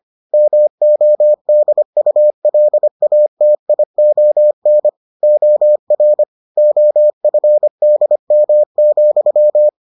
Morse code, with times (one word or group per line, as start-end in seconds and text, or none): 0.3-4.9 MODULATION
5.2-6.2 OR
6.6-9.7 OFDM,